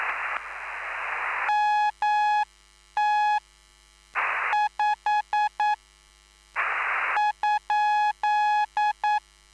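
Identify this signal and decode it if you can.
Morse code, heard as a keyed tone, about 9 wpm, 840 Hz. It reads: MT 5 ?
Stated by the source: Live CW QSO between PA2BX and SP3JFK on the 14055 kHz QRS frequency, at a basic 8 wpm.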